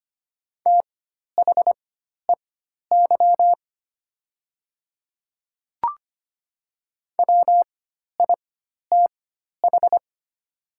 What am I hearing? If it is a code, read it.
Morse code, heard as a keyed tone, about 25 wpm, spaced as for 12 wpm, 700 Hz. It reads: THEY WITH
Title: 4x Character Spacing - Top 100 Words - 25wpm